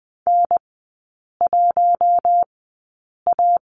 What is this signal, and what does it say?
Morse code, heard as a keyed tone, about 20 wpm, spaced as for 8 wpm, 700 Hz. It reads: N1A
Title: 041 Lesson 8e - Number 1 - Three Characters 20wpm